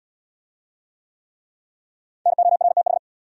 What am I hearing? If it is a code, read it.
Morse code, heard as a keyed tone, about 81 wpm, 700 Hz. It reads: NORTH